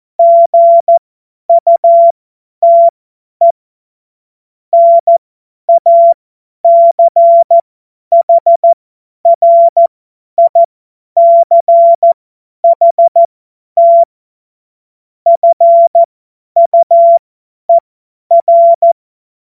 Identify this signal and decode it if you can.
Morse code, heard as a keyed tone, about 14 wpm, spaced as for 10 wpm, 680 Hz. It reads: GUTE NACHRICHT FUER
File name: Morsekurs / Witze / Witz 12 März 2024 50 F70